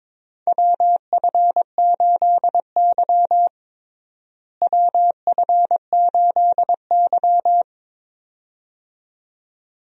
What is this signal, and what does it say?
Morse code, heard as a keyed tone, about 22 wpm, 700 Hz. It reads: WF8Y WF8Y
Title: USA Call Signs - 4 Characters - Repeated 2x - 22wpm